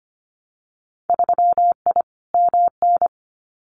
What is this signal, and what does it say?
Morse code, heard as a keyed tone, about 25 wpm, 700 Hz. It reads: 3I MN
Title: Winter Field Day Exchanges 25wpm